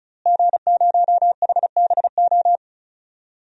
Morse code, heard as a keyed tone, about 36 wpm, 700 Hz. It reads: G0HBO
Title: DXCC-100 Call Signs - 5 Characters - Repeated 3x - 35wpm